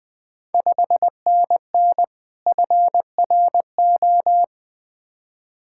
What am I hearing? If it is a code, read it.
Morse code, heard as a keyed tone, about 20 wpm, 690 Hz. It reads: 5NN FRO
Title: Ontario QSO Party Exchange 20wpm